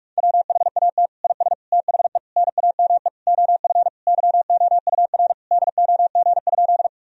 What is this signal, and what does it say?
Morse code, heard as a keyed tone, about 46 wpm, 700 Hz. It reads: WHAT IS THE NAME OF YOUR DOG?
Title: Sentences from Top 1000 Words - Review Part 2 - 45wpm